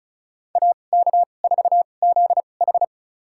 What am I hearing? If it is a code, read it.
Morse code, heard as a keyed tone, about 36 wpm, spaced as for 25 wpm, 700 Hz. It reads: AK4ZH
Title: USA Call Signs - 5 Characters - 2x Character Spacing - 35wpm